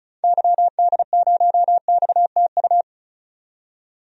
Morse code, heard as a keyed tone, about 34 wpm, 700 Hz. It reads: YD0XTU
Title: DXCC-100 Call Signs - 6 Characters - 35wpm